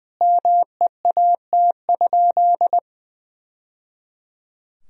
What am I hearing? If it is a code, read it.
Morse code, heard as a keyed tone, about 20 wpm, 700 Hz. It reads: MEAT?
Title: Sentences from Top 1000 Words - Review Part 3 - 20wpm